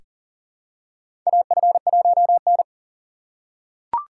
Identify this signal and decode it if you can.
Morse code, heard as a keyed tone, about 40 wpm, 690 Hz. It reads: AF1N
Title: USA Call Signs - 4 Characters - 2x Character Spacing - 40wpm